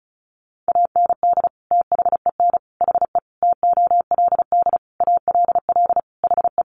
Morse code, heard as a keyed tone, about 34 wpm, 700 Hz. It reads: AND THEN HE TOLD ALL HE
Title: Sentences from Top 400 Words 35wpm